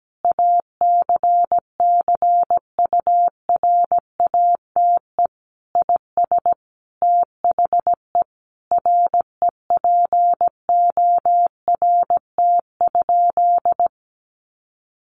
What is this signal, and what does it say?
Morse code, heard as a keyed tone, about 17 wpm, 700 Hz. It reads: ACCURATE IS THE REPORT?
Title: Sentences from Top 1200 Words 17wpm